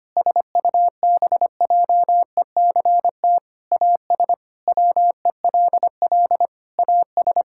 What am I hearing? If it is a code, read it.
Morse code, heard as a keyed tone, about 25 wpm, 700 Hz. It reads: SUBJECT AS WELL AS
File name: practice / 25wpm / Sentences with 4-Word Phrases - 25wpm